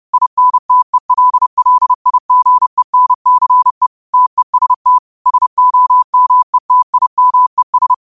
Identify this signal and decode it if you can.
Morse code, heard as a keyed tone, about 30 wpm, 1000 Hz. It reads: INTELLIGENCE TEST SOMETIMES
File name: qotdfiles / 30wpm